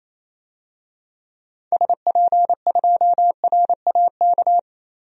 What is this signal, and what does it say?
Morse code, heard as a keyed tone, about 28 wpm, 700 Hz. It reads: SP2RAK
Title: DXCC-100 Call Signs - 6 Characters - 28wpm